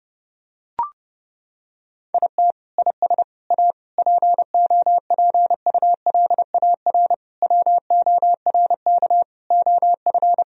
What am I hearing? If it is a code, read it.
Morse code, heard as a keyed tone, about 30 wpm, 700 Hz. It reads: IT IS A POPULAR WORK OF
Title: Sentences from Top 1000 Words - Review Part 3 - 30wpm